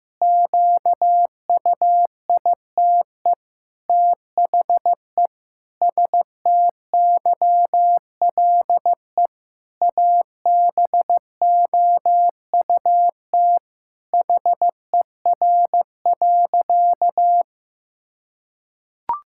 Morse code, heard as a keyed tone, about 15 wpm, 700 Hz. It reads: QUITE THE STYLE ABOUT HER.
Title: Sentences from Top 1000 Words - Review Part 5 - 15wpm